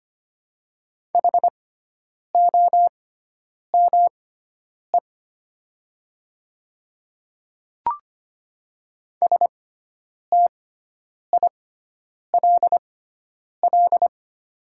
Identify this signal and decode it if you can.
Morse code, heard as a keyed tone, about 25 wpm, spaced as for 9 wpm, 700 Hz. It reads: HOME STILL
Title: 6x Character Spacing - Top 300 Words - 25wpm